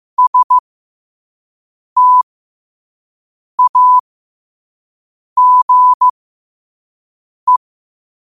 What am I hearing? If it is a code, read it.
Morse code, heard as a keyed tone, about 15 wpm, spaced as for 5 wpm, 1000 Hz. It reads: STAGE
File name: qotdfiles / 5wpm